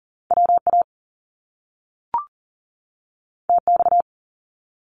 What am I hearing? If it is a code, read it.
Morse code, heard as a keyed tone, about 41 wpm, 700 Hz. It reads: WA TX
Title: US States 40wpm